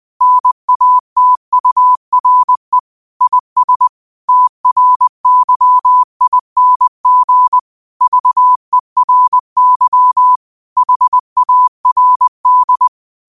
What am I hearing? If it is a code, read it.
Morse code, heard as a keyed tone, about 20 wpm, 1000 Hz. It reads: NATURE IS TRYING VERY HARD
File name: qotdfiles / 20wpm